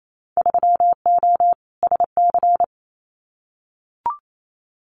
Morse code, heard as a keyed tone, about 28 wpm, 700 Hz. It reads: 3O SC
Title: Winter Field Day Exchanges Encore 28wpm